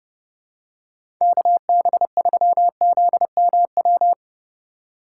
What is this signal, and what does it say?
Morse code, heard as a keyed tone, about 30 wpm, 700 Hz. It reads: KB3ZMW
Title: USA Call Signs - 2x Character Spacing - 30wpm